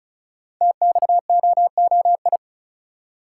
Morse code, heard as a keyed tone, about 33 wpm, 700 Hz. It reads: TXOOI
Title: Code Groups - 5 Letters - 35wpm